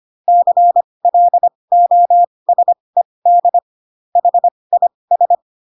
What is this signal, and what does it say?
Morse code, heard as a keyed tone, about 25 wpm, spaced as for 20 wpm, 700 Hz. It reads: CLOSED HIS
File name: Pinocchio / Pinocchio ch8 15 25 ik1hge